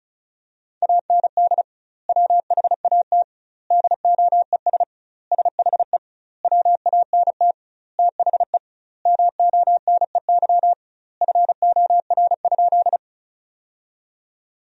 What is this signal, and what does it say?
Morse code, heard as a keyed tone, about 34 wpm, 700 Hz. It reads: AND WHAT DOES SHE WANT THE MONEY FOR?
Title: 2x Word-Spacing - Sentences from Top 400 Words - 35wpm